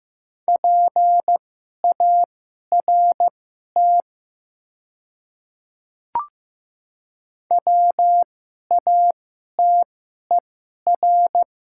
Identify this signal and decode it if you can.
Morse code, heard as a keyed tone, about 15 wpm, spaced as for 11 wpm, 700 Hz. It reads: PART WATER
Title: 2x Character Spacing - Top 300 Words - 15wpm